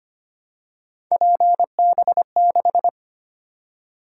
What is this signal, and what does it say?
Morse code, heard as a keyed tone, about 25 wpm, 700 Hz. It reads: PB6
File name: Three Letter-Number Combinations - 25wpm